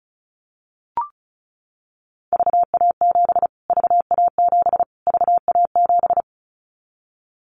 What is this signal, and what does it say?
Morse code, heard as a keyed tone, about 34 wpm, 700 Hz. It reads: VA7 VA7 VA7
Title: Callsign Prefixes for Top 25 DXCC - Repeated 3x - 35wpm